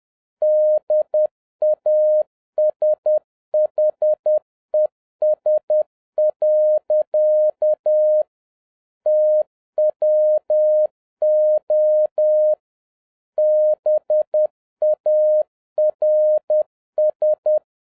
Morse code, heard as a keyed tone, about 10 wpm, 610 Hz. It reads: DASHES. TWO BARS